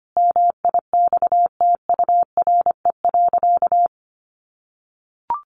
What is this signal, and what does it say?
Morse code, heard as a keyed tone, about 25 wpm, 700 Hz. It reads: MIXTURE.
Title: Sentences from Top 1500 Words 25wpm